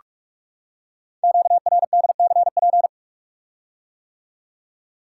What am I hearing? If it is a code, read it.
Morse code, heard as a keyed tone, about 44 wpm, 700 Hz. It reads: QRDCP
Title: Code Groups - 5 Letter:Numbers - 45wpm